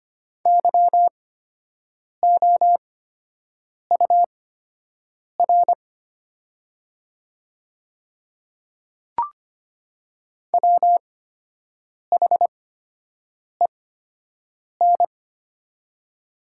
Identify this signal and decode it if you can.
Morse code, heard as a keyed tone, about 25 wpm, spaced as for 7 wpm, 700 Hz. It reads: YOUR WHEN
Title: 8x Character Spacing - Top 100 Words - 25wpm